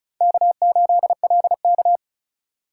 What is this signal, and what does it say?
Morse code, heard as a keyed tone, about 45 wpm, 700 Hz. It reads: K8LK